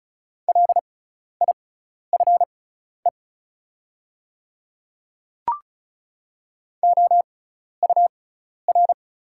Morse code, heard as a keyed tone, about 34 wpm, spaced as for 12 wpm, 690 Hz. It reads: LIFE OUR
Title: 6x Character Spacing - Top 300 Words - 35wpm